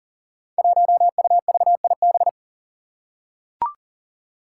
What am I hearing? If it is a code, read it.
Morse code, heard as a keyed tone, about 40 wpm, 700 Hz. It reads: 1UVIB
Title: Five Challenging Letter-Number Combinations 40wpm